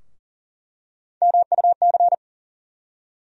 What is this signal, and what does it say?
Morse code, heard as a keyed tone, about 40 wpm, 700 Hz. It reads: MUC
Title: USA Call Sign Postfixes - 3 Letters 40wpm